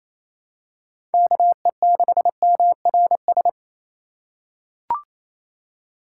Morse code, heard as a keyed tone, about 28 wpm, 700 Hz. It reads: KE6MRS
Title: USA Call Signs - 6 Characters - 8x Character Spacing - 28wpm